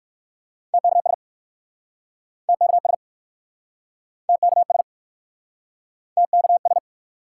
Morse code, heard as a keyed tone, about 56 wpm, 700 Hz. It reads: TKS TKS TKS TKS